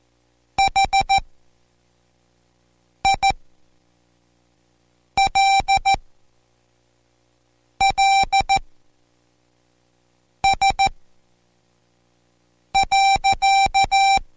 Morse code, heard as a keyed tone, about 14 wpm, spaced as for 4 wpm, 780 Hz. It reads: HILLS.